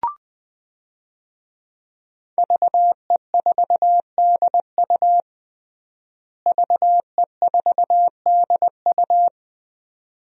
Morse code, heard as a keyed tone, about 20 wpm, 700 Hz. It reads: VE4DU VE4DU
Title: DXCC-100 Call Signs - 5 Characters - Repeated 2x - 20wpm